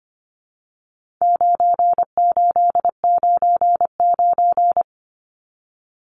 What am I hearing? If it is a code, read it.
Morse code, heard as a keyed tone, about 25 wpm, 700 Hz. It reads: 9899
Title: Numbers - 4-Digit 25wpm